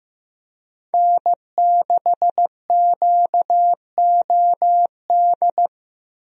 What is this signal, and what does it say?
Morse code, heard as a keyed tone, about 16 wpm, 700 Hz. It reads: N6QOD
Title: USA Call Signs - 5 Characters - 15wpm